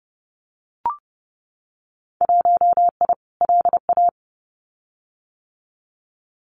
Morse code, heard as a keyed tone, about 31 wpm, 700 Hz. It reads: 1I LA
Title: Winter Field Day Exchanges Encore 30wpm